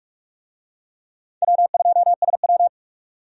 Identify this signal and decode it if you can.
Morse code, heard as a keyed tone, about 45 wpm, 700 Hz. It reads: W2SW
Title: USA Call Signs - 4 Characters - 45wpm